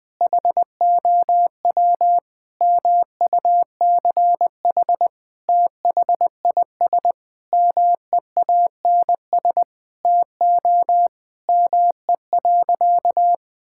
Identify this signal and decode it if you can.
Morse code, heard as a keyed tone, about 20 wpm, 700 Hz. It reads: HOW MUCH THIS MEANS TO ME.
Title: Sentences with 5-Word Phrases - Spoken Beginning - 20wpm